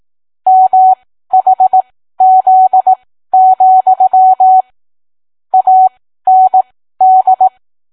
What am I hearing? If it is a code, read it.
Morse code, heard as a keyed tone, about 18 wpm, spaced as for 13 wpm, 750 Hz. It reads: MHZ, AND